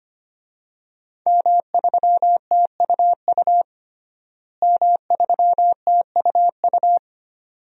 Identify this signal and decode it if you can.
Morse code, heard as a keyed tone, about 25 wpm, 700 Hz. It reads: M3TUU M3TUU
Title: DXCC-100 Call Signs - 5 Characters - Repeated 2x - 25wpm